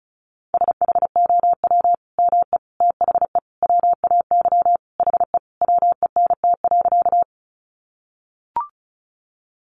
Morse code, heard as a keyed tone, about 34 wpm, 700 Hz. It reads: SHOW ME THE WAY HE WENT.
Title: Sentences from Top 200 Words 35wpm